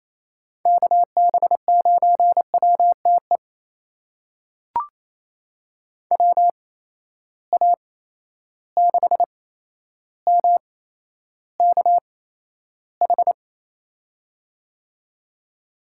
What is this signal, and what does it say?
Morse code, heard as a keyed tone, about 27 wpm, spaced as for 12 wpm, 700 Hz. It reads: KB9WTE WA6MKH
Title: USA Call Signs - 6 Characters - 8x Character Spacing - 28wpm